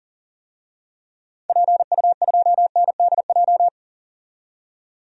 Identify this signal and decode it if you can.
Morse code, heard as a keyed tone, about 40 wpm, 700 Hz. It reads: PU2NDJ
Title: DXCC-100 Call Signs - 6 Characters - 40wpm